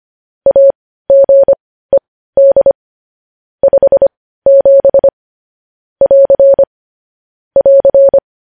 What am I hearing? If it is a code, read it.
Morse code, heard as a keyed tone, about 25 wpm, spaced as for 15 wpm, 550 Hz. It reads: AGED 57 <AR> <AR>
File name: podcast / News headlines-1x-25x15